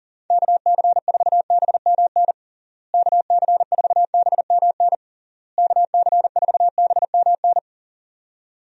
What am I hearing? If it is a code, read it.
Morse code, heard as a keyed tone, about 40 wpm, 700 Hz. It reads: KC4BMN KC4BMN KC4BMN
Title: USA Call Signs - 6 Characters - Repeated 3x - 40wpm